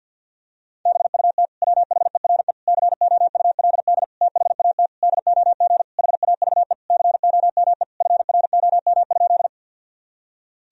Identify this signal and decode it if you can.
Morse code, heard as a keyed tone, about 50 wpm, 700 Hz. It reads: BUT WHERE COULD THAT DOG HAVE COME FROM?
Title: Sentences from Top 1000 Words - Review Part 2 - 50wpm